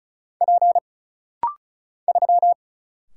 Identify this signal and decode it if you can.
Morse code, heard as a keyed tone, about 33 wpm, 700 Hz. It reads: P 3
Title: Single Letter-Number Alt - Rapid-Fire 35wpm